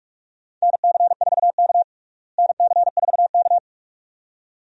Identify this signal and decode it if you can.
Morse code, heard as a keyed tone, about 46 wpm, 700 Hz. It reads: NC4K NC4K
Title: USA Call Signs - 4 Characters - Repeated 2x - 45wpm